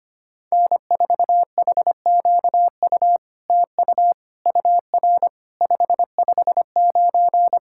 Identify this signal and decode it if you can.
Morse code, heard as a keyed tone, about 25 wpm, 700 Hz. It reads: N4HQU TU UR 559
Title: SOTA QSO Practice 25wpm